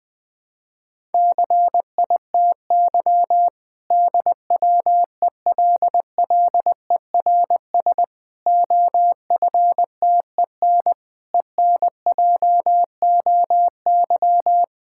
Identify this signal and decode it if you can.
Morse code, heard as a keyed tone, about 20 wpm, 700 Hz. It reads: CITY DWELLERS OFTEN ENJOY
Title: Difficult Sentences with Hints - 20wpm